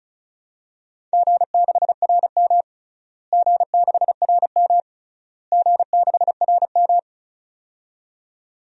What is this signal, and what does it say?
Morse code, heard as a keyed tone, about 36 wpm, 700 Hz. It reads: G6RM G6RM G6RM
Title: DXCC-100 Call Signs - 4 Characters - Repeated 3x - 35wpm